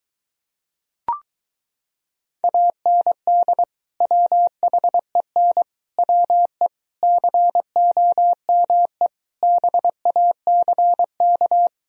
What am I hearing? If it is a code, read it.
Morse code, heard as a keyed tone, about 23 wpm, 700 Hz. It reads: AND WHEN WE COME BACK